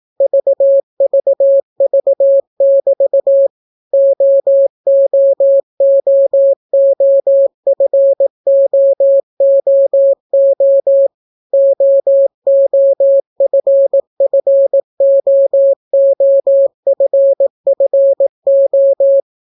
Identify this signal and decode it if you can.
Morse code, heard as a keyed tone, about 18 wpm, 550 Hz. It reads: VVV<BT> OOOOFOOO OOFFOOFFO